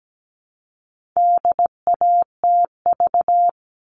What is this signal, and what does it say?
Morse code, heard as a keyed tone, about 17 wpm, 700 Hz. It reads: DATV
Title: All Cut Numbers - 4-Digit 17wpm